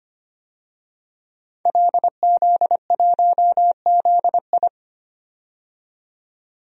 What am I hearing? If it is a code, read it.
Morse code, heard as a keyed tone, about 25 wpm, 700 Hz. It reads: LZ1ZI
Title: DXCC-100 Call Signs - 5 Characters - 25wpm